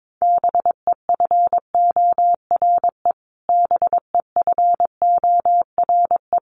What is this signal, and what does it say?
Morse code, heard as a keyed tone, about 22 wpm, 700 Hz. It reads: BEFORE BEFORE
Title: Top 300 Words Repeated 22wpm